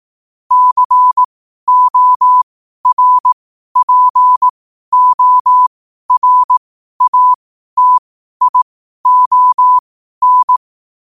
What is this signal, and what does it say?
Morse code, heard as a keyed tone, about 18 wpm, spaced as for 12 wpm, 1000 Hz. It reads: CORPORATION
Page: Quotes for Fri, 15 Aug 2025 in Morse Code at 12 words per minute.